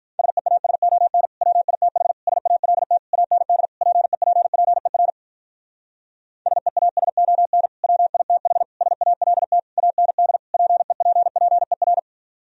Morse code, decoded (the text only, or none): SEASON WITH SALT AND PEPPER SEASON WITH SALT AND PEPPER